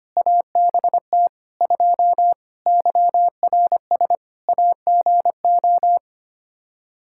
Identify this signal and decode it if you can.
Morse code, heard as a keyed tone, about 25 wpm, 700 Hz. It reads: ABT 2 YRS AGO
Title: Practice QSO - Exchange 3 - More Info - 25wpm